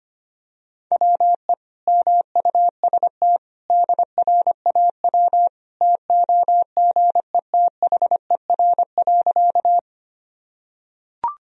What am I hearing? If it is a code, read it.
Morse code, heard as a keyed tone, about 25 wpm, 700 Hz. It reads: WE MUST DRAW TOGETHER.
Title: Sentences from Top 1000 Words - Review Part 1 - 25wpm